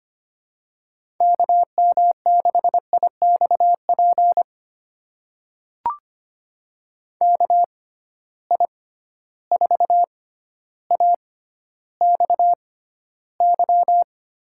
Morse code, heard as a keyed tone, about 25 wpm, spaced as for 13 wpm, 700 Hz. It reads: KM6IXP KI4AXY
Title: USA Call Signs - 6 Characters - 6x Character Spacing - 25wpm